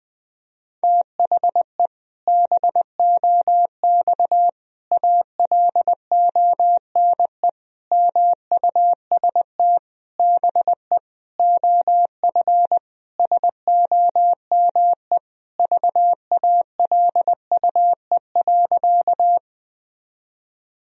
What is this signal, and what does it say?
Morse code, heard as a keyed tone, about 20 wpm, 700 Hz. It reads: THE BOX ALONE MUST BE OF SOME VALUE.
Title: Sentences from Top 1000 Words - Review Part 3 - 20wpm